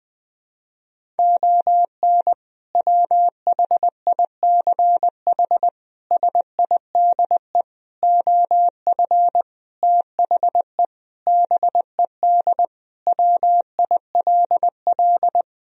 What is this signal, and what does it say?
Morse code, heard as a keyed tone, about 20 wpm, 700 Hz. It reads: ON WHICH SIDE OF THE BED WILL
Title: Sentences from Top 1000 Words - Review Part 1 - 20wpm